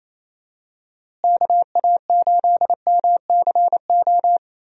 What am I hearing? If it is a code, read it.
Morse code, heard as a keyed tone, about 28 wpm, 700 Hz. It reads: KA8MCO
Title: USA Call Signs - 6 Characters - Repeated 3x - 28wpm